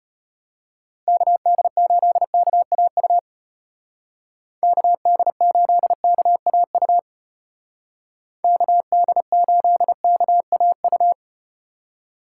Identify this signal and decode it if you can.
Morse code, heard as a keyed tone, about 34 wpm, 700 Hz. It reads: KD8KAU KD8KAU KD8KAU